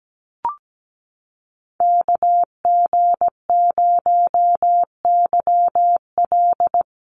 Morse code, heard as a keyed tone, about 17 wpm, 700 Hz. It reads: KG0YL
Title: Call Signs - 17wpm